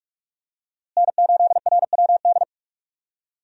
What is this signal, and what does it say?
Morse code, heard as a keyed tone, about 46 wpm, 700 Hz. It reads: N8RWD
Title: USA Call Signs - 5 Characters - Repeated 2x - 45wpm